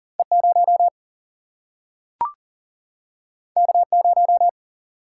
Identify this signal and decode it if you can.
Morse code, heard as a keyed tone, about 39 wpm, 700 Hz. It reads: E 0 K 0
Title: Single-Letter Single-Number 40wpm